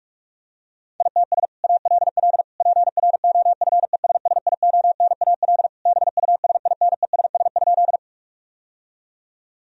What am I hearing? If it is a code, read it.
Morse code, heard as a keyed tone, about 46 wpm, 700 Hz. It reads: ITS ALL PROFESSIONAL BUSINESS?
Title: Sentences from Top 1000 Words - Review Part 4 - 45wpm